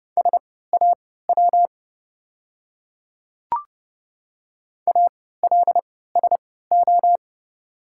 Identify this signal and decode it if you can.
Morse code, heard as a keyed tone, about 30 wpm, spaced as for 17 wpm, 700 Hz. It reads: SAW ALSO
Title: 3x Character Spacing - Top 300 Words - 30wpm